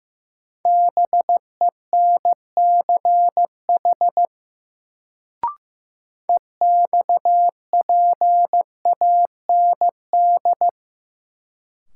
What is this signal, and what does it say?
Morse code, heard as a keyed tone, about 15 wpm, 700 Hz. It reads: BENCH EXPAND
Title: Sentences from Top 1400 Words - New Word Drill - 15wpm